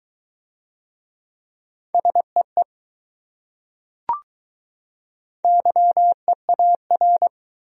Morse code, heard as a keyed tone, about 20 wpm, 700 Hz. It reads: SEE YEAR